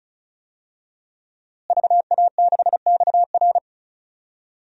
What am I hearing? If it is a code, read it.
Morse code, heard as a keyed tone, about 36 wpm, 700 Hz. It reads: VA6XR
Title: DXCC-100 Call Signs - 5 Characters - Repeated 3x - 35wpm